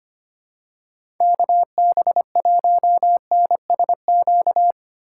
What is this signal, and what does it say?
Morse code, heard as a keyed tone, about 25 wpm, 700 Hz. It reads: KB1NSQ
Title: USA Call Signs - 2x Character Spacing - 25wpm